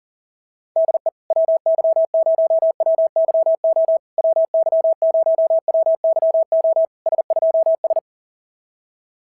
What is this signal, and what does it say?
Morse code, heard as a keyed tone, about 37 wpm, 650 Hz. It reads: DE WY0WYO WY0WYO S2S
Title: SOTA QSO Practice 40wpm